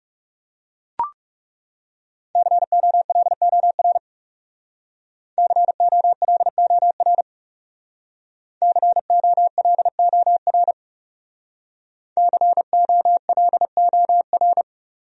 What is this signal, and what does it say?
Morse code, heard as a keyed tone, about 38 wpm, 700 Hz. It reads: COLOR COLOR COLOR COLOR